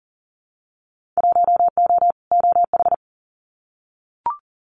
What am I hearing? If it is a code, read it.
Morse code, heard as a keyed tone, about 38 wpm, 700 Hz. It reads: 1O OH
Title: Winter Field Day Exchanges 40wpm